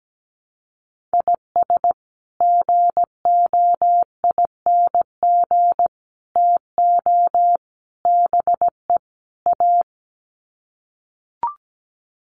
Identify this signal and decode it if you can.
Morse code, heard as a keyed tone, about 17 wpm, 700 Hz. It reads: IS GOING TO BE A
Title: Sets of 5 Words 17wpm